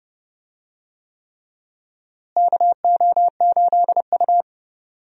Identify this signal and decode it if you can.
Morse code, heard as a keyed tone, about 28 wpm, 700 Hz. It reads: KO8U
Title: USA Call Signs - 4 Characters - Repeated 3x - 30wpm